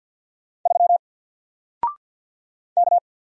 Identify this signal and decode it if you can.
Morse code, heard as a keyed tone, about 50 wpm, 700 Hz. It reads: 3 K
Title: Single Character (Letter:Number:Punctuation:Prosign) Alt 50wpm